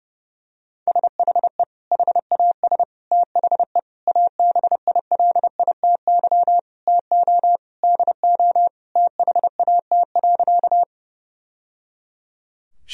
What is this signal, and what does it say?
Morse code, heard as a keyed tone, about 30 wpm, 700 Hz. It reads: SHE HAS THE ABILITY TO DO THAT.
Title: Sentences from Top 1000 Words - Review Part 3 - 30wpm